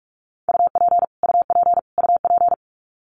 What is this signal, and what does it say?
Morse code, heard as a keyed tone, about 44 wpm, 700 Hz. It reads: UP UP UP
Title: Top 200 Words Repeated 45wpm